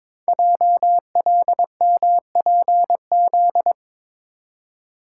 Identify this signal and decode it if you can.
Morse code, heard as a keyed tone, about 22 wpm, 700 Hz. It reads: JLMPZ
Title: Code Groups - 5 Letters - 22wpm